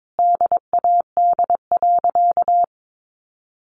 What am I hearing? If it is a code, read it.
Morse code, heard as a keyed tone, about 22 wpm, 700 Hz. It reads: DAD.
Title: Additional Easy Phrases 22wpm